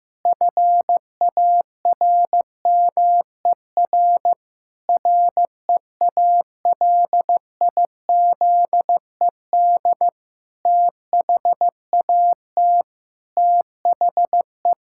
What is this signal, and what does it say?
Morse code, heard as a keyed tone, about 14 wpm, 700 Hz. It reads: FARMER REALIZED THAT THE
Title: Pangram Challenge v2 - 15wpm